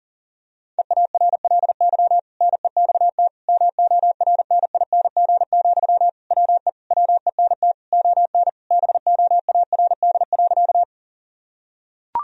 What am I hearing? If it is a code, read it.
Morse code, heard as a keyed tone, about 40 wpm, 700 Hz. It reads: EARLY NEXT MORNING, WE WENT ON BOARD.
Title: Sentences from Top 1000 Words - Review Part 4 - 40wpm